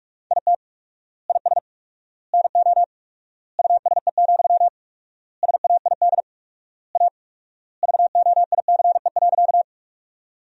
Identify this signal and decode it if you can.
Morse code, heard as a keyed tone, about 47 wpm, 700 Hz. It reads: IT IS NO USE, SAID A VOICE.
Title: 4x Word-Spacing - Sentences from Top 500 Words - 45wpm